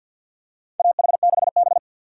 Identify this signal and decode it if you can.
Morse code, heard as a keyed tone, about 51 wpm, 700 Hz. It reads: AH6B
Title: USA Call Signs - 4 Characters - 2x Character Spacing - 50wpm